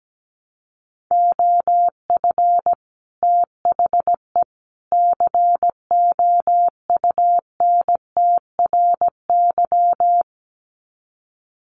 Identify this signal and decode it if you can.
Morse code, heard as a keyed tone, about 17 wpm, 700 Hz. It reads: OF THE COUNTRY
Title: Sets of 3 Words 17wpm